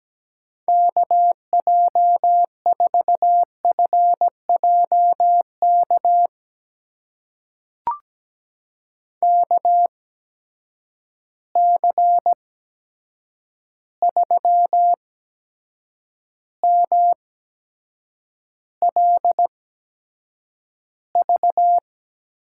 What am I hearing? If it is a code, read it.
Morse code, heard as a keyed tone, about 17 wpm, spaced as for 7 wpm, 700 Hz. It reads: KJ4FJK KC3MLV
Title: USA Call Signs - 6 Characters - 8x Character Spacing - 17wpm